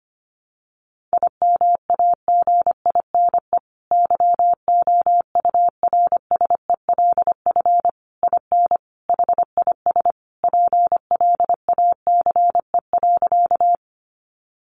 Morse code, heard as a keyed tone, about 25 wpm, 700 Hz. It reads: IMAGINE YOURSELF IN HIS PLACE.
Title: Sentences from Top 1400 Words 25wpm